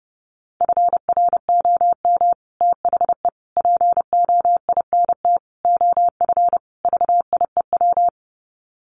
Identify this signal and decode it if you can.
Morse code, heard as a keyed tone, about 30 wpm, 700 Hz. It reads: FROM THE POINT OF VIEW
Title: Sets of 5 Words 30wpm